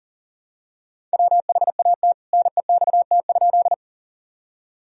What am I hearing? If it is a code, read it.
Morse code, heard as a keyed tone, about 40 wpm, 700 Hz. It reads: WHAT NEXT?
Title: Sentences from Top 1000 Words - Review Part 1 - 40wpm